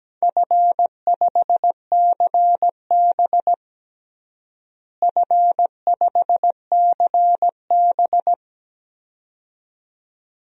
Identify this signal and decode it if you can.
Morse code, heard as a keyed tone, about 17 wpm, 700 Hz. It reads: F5CB F5CB
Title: DXCC-100 Call Signs - 4 Characters - Repeated 2x - 17wpm